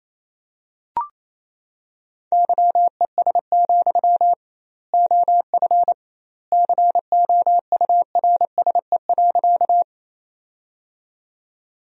Morse code, heard as a keyed tone, about 28 wpm, 700 Hz. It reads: YES, OF COURSE.
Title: 2x Word-Spacing - Sentences from Top 500 Words - 28wpm